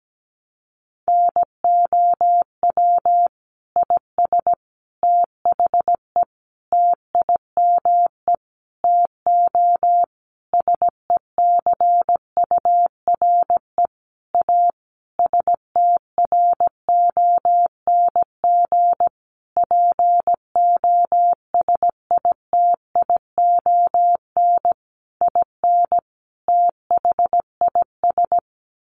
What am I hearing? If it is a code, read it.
Morse code, heard as a keyed tone, about 17 wpm, 700 Hz. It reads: NOW IS THE TIME TO SECURE A STRONG POSITION IN THIS
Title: Sentences from Top 1200 Words 17wpm